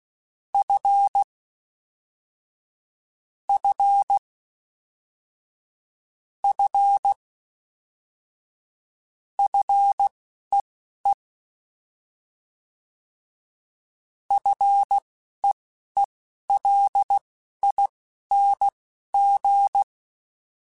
Morse code, heard as a keyed tone, about 16 wpm, spaced as for 11 wpm, 800 Hz. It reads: F F F FEE FEELING